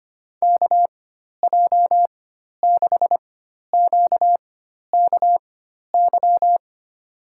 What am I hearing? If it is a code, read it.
Morse code, heard as a keyed tone, about 25 wpm, spaced as for 12 wpm, 700 Hz. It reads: KJ6QKY
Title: USA Call Signs - 6 Characters - 4x Character Spacing - 25wpm